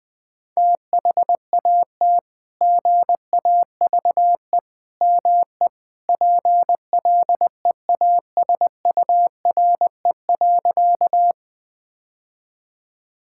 Morse code, heard as a keyed tone, about 19 wpm, 700 Hz. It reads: THAT GAVE ME PLEASURE.
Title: Sentences from Top 1600 Words 20wpm